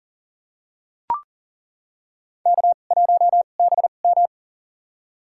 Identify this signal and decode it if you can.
Morse code, heard as a keyed tone, about 37 wpm, spaced as for 28 wpm, 700 Hz. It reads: K1BM
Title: USA Call Signs - 4 Characters - 2x Character Spacing - 40wpm